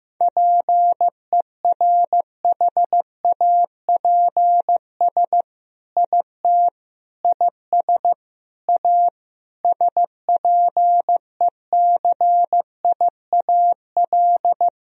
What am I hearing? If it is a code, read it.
Morse code, heard as a keyed tone, about 15 wpm, 700 Hz. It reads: PERHAPS IT IS A SPECIAL
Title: Sentences from Top 1000 Words - Review Part 3 - 15wpm